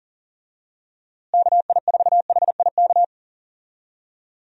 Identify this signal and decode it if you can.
Morse code, heard as a keyed tone, about 40 wpm, 700 Hz. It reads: KI4HIK
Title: USA Call Signs - 6 Characters - Repeated 3x - 40wpm